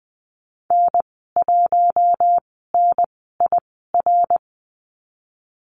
Morse code, heard as a keyed tone, about 20 wpm, spaced as for 15 wpm, 700 Hz. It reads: N1NIR
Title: 055 Lesson 10g - Letter r - Call Signs 20wpm